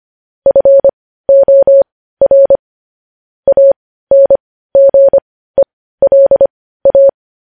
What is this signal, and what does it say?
Morse code, heard as a keyed tone, about 25 wpm, spaced as for 15 wpm, 550 Hz. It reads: FOR ANGELA
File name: podcast / News headlines-1x-25x15